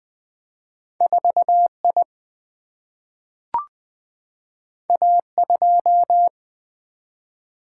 Morse code, heard as a keyed tone, about 21 wpm, 700 Hz. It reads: 4 I A 2
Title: Single-Letter Single-Number 20wpm